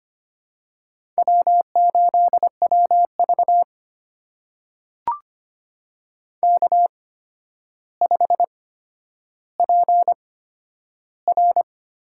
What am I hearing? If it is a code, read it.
Morse code, heard as a keyed tone, about 25 wpm, spaced as for 11 wpm, 700 Hz. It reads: W8WV K5PR
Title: USA Call Signs - 8x Character Spacing - 25wpm